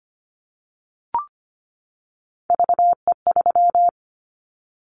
Morse code, heard as a keyed tone, about 25 wpm, 700 Hz. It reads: VE3
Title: Callsign Prefixes for Top 50 DXCC - 25wpm